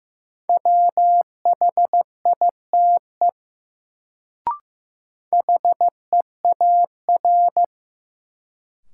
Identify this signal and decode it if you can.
Morse code, heard as a keyed tone, about 15 wpm, 700 Hz. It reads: WHITE HEAR
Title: Sentences from Top 300 Words - New Word Drill - 15wpm